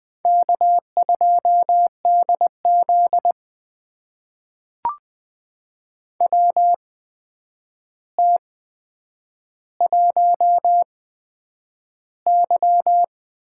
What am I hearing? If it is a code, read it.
Morse code, heard as a keyed tone, about 20 wpm, spaced as for 9 wpm, 700 Hz. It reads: K2DZ WT1Y
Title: USA Call Signs - 8x Character Spacing - 20wpm